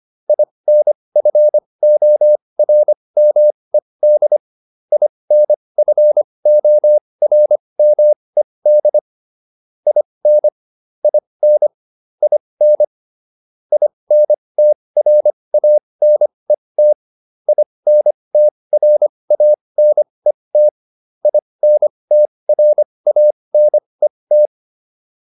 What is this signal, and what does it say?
Morse code, heard as a keyed tone, about 25 wpm, spaced as for 20 wpm, 600 Hz. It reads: INFORMED INFORMED IN IN IN INTRANET INTRANET INTRANET